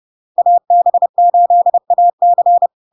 Morse code, heard as a keyed tone, about 30 wpm, 700 Hz. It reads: AB8AC
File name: LARK-FD-2022-Log-in-CW-30wpm-700-Hz